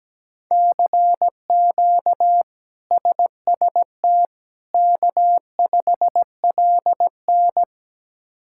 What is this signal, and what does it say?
Morse code, heard as a keyed tone, about 17 wpm, 700 Hz. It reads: CQ SST K5LN
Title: K1USN SST QSO Practice - 17wpm